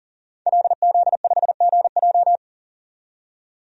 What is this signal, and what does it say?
Morse code, heard as a keyed tone, about 38 wpm, 700 Hz. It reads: LZ5GJ
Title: DXCC-100 Call Signs - 5 Characters - 40wpm